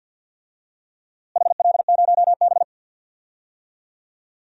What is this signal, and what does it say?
Morse code, heard as a keyed tone, about 51 wpm, 700 Hz. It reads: HL0B
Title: DXCC-100 Call Signs - 4 Characters - Repeated 2x - 50wpm